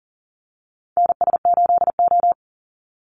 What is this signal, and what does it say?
Morse code, heard as a keyed tone, about 40 wpm, 700 Hz. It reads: NS8O
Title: Call Signs - 40wpm